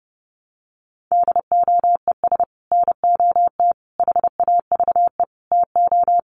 Morse code, heard as a keyed tone, about 30 wpm, 700 Hz. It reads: DOES NOT HAVE TO
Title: Sets of 5 Words 30wpm